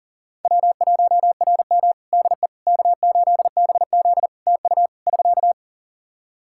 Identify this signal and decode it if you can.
Morse code, heard as a keyed tone, about 40 wpm, 700 Hz. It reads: W1RM DE K8BZ TU <SK>
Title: Practice QSO - Exchange 4 - Ending - 40wpm